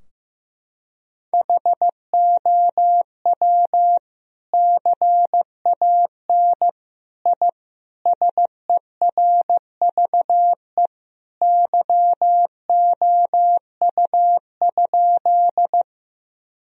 Morse code, heard as a keyed tone, about 15 wpm, 700 Hz. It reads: HOW CAN I SERVE YOU?
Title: Sentences from Top 1000 Words - Review Part 2 - 15wpm